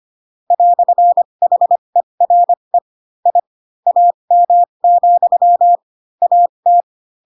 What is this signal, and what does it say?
Morse code, heard as a keyed tone, about 25 wpm, spaced as for 22 wpm, 700 Hz. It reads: "HERE I AM, AT
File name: Pinocchio / Pinocchio ch2 18 25 ik1hge